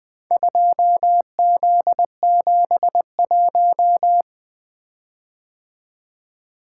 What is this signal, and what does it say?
Morse code, heard as a keyed tone, about 20 wpm, 700 Hz. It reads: 2Z71
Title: Four Letter-Number Combinations - 20wpm